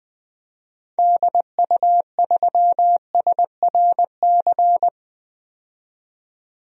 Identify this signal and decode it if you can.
Morse code, heard as a keyed tone, about 20 wpm, 700 Hz. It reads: DU3SRC
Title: DXCC-100 Call Signs - 6 Characters - 20wpm